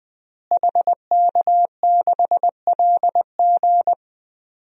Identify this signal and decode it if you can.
Morse code, heard as a keyed tone, about 20 wpm, 700 Hz. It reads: HK6LG
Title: DXCC-100 Call Signs - 5 Characters - Repeated 3x - 20wpm